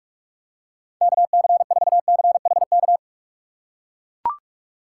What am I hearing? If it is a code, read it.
Morse code, heard as a keyed tone, about 42 wpm, 700 Hz. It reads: KC4CHK
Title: USA Call Signs - 6 Characters - 4x Character Spacing - 45wpm